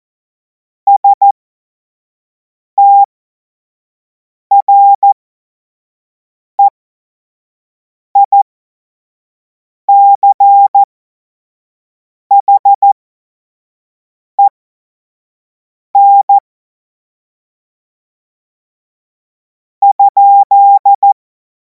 Morse code, heard as a keyed tone, about 14 wpm, spaced as for 4 wpm, 800 Hz. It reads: STREICHEN?
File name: Morsekurs / Witze / Witz 10. Februar 2026 25 F70